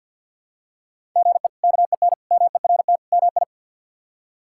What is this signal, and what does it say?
Morse code, heard as a keyed tone, about 51 wpm, 700 Hz. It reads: GE KEN MERT MI
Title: K1USN SST QSO Practice - 50wpm